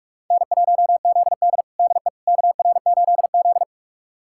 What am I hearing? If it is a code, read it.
Morse code, heard as a keyed tone, about 47 wpm, 700 Hz. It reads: N1ZD DE KR8Z
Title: USA Callsign DE Callsign - 4 Characters - 45wpm